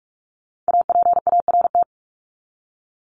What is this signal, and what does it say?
Morse code, heard as a keyed tone, about 44 wpm, 700 Hz. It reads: APART
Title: Top 1501-1750 Words 45wpm